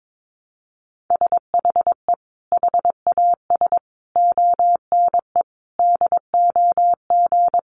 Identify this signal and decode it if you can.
Morse code, heard as a keyed tone, about 22 wpm, 700 Hz. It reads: SHE HAS ONE DOG
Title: Easy Phrases 22wpm